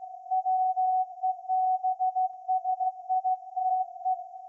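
Morse code, heard as a keyed tone, about 16 wpm, 740 Hz. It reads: WEBSITE
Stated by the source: Morse code from SAQ, received at a station in Putignano, Apulia, Italy, on VLF.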